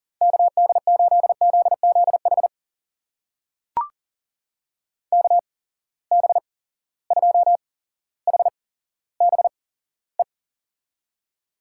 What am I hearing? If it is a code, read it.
Morse code, heard as a keyed tone, about 42 wpm, spaced as for 17 wpm, 700 Hz. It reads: KD8ZZH KB2HBE
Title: USA Call Signs - 6 Characters - 8x Character Spacing - 40wpm